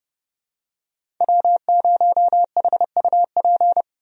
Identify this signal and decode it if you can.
Morse code, heard as a keyed tone, about 31 wpm, 700 Hz. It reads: W0HUP
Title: USA Call Signs - 5 Characters - 30wpm